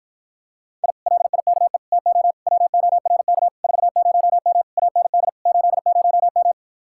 Morse code, heard as a keyed tone, about 52 wpm, 700 Hz. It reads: I LIKE TO WORK 40M AND 80M